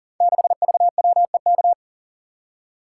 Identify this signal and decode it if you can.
Morse code, heard as a keyed tone, about 38 wpm, 700 Hz. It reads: 6VWEK
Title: Code Groups - 5 Letter:Numbers - 40wpm